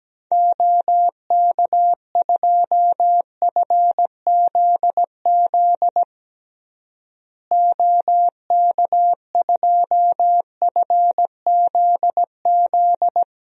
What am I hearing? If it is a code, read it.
Morse code, heard as a keyed tone, about 17 wpm, 700 Hz. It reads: OK2FZZ OK2FZZ
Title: DXCC-100 Call Signs - 6 Characters - Repeated 3x - 17wpm